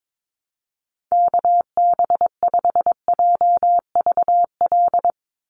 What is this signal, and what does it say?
Morse code, heard as a keyed tone, about 22 wpm, 700 Hz. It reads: KB5JVL
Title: USA Call Signs 22wpm